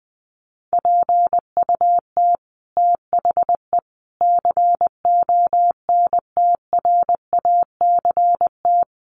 Morse code, heard as a keyed tone, about 20 wpm, 700 Hz. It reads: PUT THE CONTRACT
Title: Sentences from Top 1400 Words 20wpm